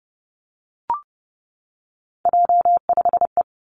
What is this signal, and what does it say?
Morse code, heard as a keyed tone, about 30 wpm, 700 Hz. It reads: J5E
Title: Three Challenging Letter-Number Combinations 30wpm